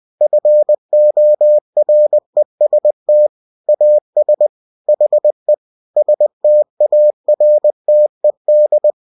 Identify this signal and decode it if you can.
Morse code, heard as a keyed tone, about 20 wpm, 600 Hz. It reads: FOREST AS HE STARTED